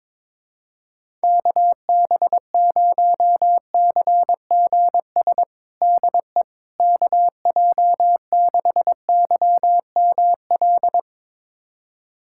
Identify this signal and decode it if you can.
Morse code, heard as a keyed tone, about 22 wpm, 700 Hz. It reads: KB0CGS DE KJ6YML
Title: USA Callsign DE Callsign - 6 Characters - 22wpm